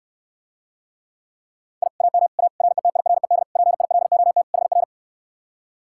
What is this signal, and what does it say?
Morse code, heard as a keyed tone, about 69 wpm, 700 Hz. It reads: I AM A RETIRED PILOT HR